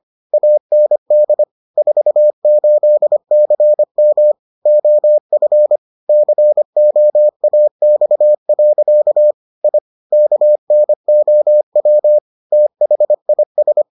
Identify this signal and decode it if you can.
Morse code, heard as a keyed tone, about 25 wpm, 600 Hz. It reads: AND 48CM OF COAX. I KNOW THIS